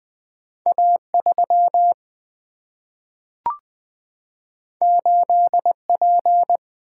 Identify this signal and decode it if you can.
Morse code, heard as a keyed tone, about 20 wpm, 700 Hz. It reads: A 3 8 P
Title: Single-Letter Single-Number 20wpm